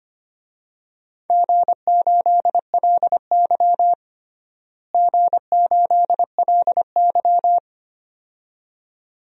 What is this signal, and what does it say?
Morse code, heard as a keyed tone, about 25 wpm, 700 Hz. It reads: G8LY G8LY
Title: DXCC-100 Call Signs - 4 Characters - Repeated 2x - 25wpm